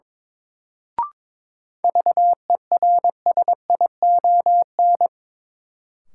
Morse code, heard as a keyed tone, about 22 wpm, 700 Hz. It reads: VERSION
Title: Sentences from Top 1400 Words - New Word Drill Alt - 22wpm